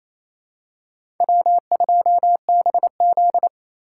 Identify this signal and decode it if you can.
Morse code, heard as a keyed tone, about 28 wpm, 700 Hz. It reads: W2BZ
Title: USA Call Signs - 4x Character Spacing - 28wpm